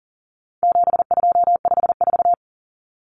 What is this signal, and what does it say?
Morse code, heard as a keyed tone, about 40 wpm, 700 Hz. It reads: 7254
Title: Numbers - 4-Digit 40wpm